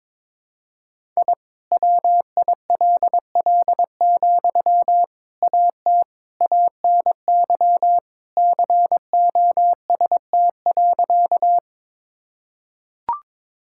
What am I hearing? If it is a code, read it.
Morse code, heard as a keyed tone, about 22 wpm, 700 Hz. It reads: I WILL, AT ANY COST.
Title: Sentences from Top 1000 Words - Review Part 5 - 22wpm